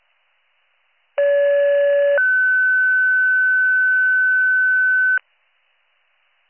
Two-tone page, heard as a one-second tone, then a three-second tone, about 570 then 1500 Hz. It is heard from 1.2 to 5.2 s.